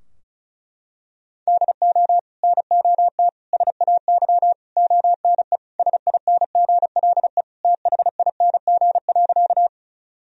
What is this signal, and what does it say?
Morse code, heard as a keyed tone, about 36 wpm, 700 Hz. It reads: DO NOT SAY ONE SINGLE THING.
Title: Sentences from Top 1000 Words - Review Part 4 - 35wpm